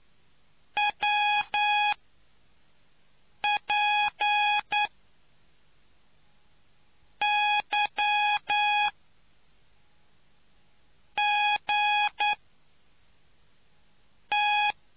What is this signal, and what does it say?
Morse code, heard as a keyed tone, about 9 wpm, spaced as for 4 wpm, 830 Hz. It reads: WPYGT